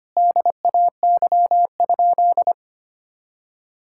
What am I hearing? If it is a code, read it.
Morse code, heard as a keyed tone, about 25 wpm, 700 Hz. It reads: DAY?
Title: Sentences from Top 1000 Words - Review Part 6 - 25wpm